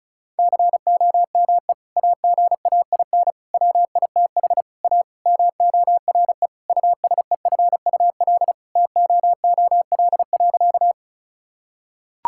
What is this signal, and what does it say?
Morse code, heard as a keyed tone, about 34 wpm, 700 Hz. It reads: COME AGAIN WITH A MORE USEFUL TOOL.
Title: Sentences from Top 1000 Words - Review Part 6 - 35wpm